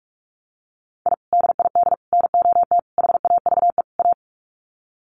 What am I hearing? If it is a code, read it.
Morse code, heard as a keyed tone, about 44 wpm, 700 Hz. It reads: I DID NOT HAVE A
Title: Sets of 5 Words 45wpm